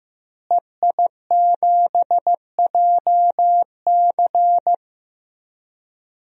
Morse code, heard as a keyed tone, about 15 wpm, 700 Hz. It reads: EI7JC
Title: DXCC-100 Call Signs - 5 Characters - Repeated 2x - 15wpm